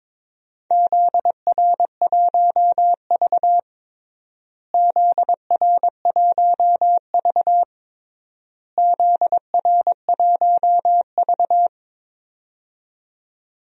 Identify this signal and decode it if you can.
Morse code, heard as a keyed tone, about 22 wpm, 700 Hz. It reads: ZR1V ZR1V ZR1V
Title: DXCC-100 Call Signs - 4 Characters - Repeated 3x - 22wpm